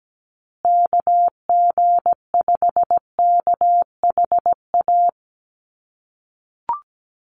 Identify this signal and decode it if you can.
Morse code, heard as a keyed tone, about 17 wpm, 700 Hz. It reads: KG5KHA
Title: USA Call Signs Repeated 3x 17wpm